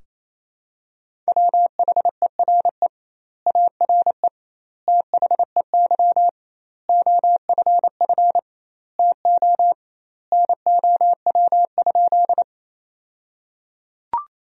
Morse code, heard as a keyed tone, about 28 wpm, 700 Hz. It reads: WHERE ARE THEY OFF TO NOW?
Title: 2x Word-Spacing - Sentences from Top 200 Words - 28wpm